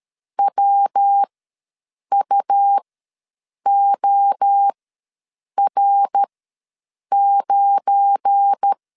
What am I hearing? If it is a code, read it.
Morse code, heard as a keyed tone, about 13 wpm, spaced as for 7 wpm, 800 Hz. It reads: WUOR9
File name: LEZIONI / 10 SIGNS / 10C DOT ws 12wpm 3548